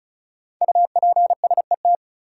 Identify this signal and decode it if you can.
Morse code, heard as a keyed tone, about 36 wpm, 700 Hz. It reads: UPSET
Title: Sentences from Top 800 Words - New Word Drill - 35wpm